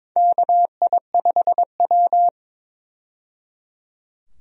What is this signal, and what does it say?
Morse code, heard as a keyed tone, about 22 wpm, 700 Hz. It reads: KI5W
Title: USA Call Signs - 4 Characters - 22wpm